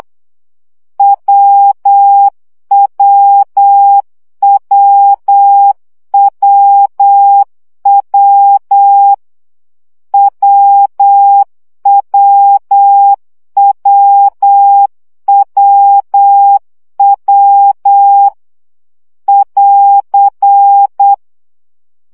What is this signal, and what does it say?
Morse code, heard as a keyed tone, about 8 wpm, 800 Hz. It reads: WWWWW WWWWW <AR>
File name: CW gyakorlas w 7 baud